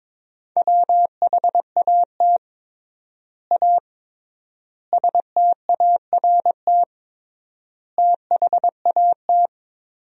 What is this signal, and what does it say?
Morse code, heard as a keyed tone, about 22 wpm, 700 Hz. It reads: WHAT A START THAT
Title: 3x Word-Spacing - Sentences from Top 300 Words - 22wpm